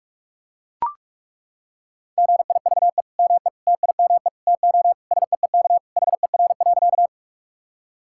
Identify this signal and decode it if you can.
Morse code, heard as a keyed tone, about 41 wpm, 700 Hz. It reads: GIVE ME TIME TO SEEK HER.
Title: Sentences from Top 1600 Words 45wpm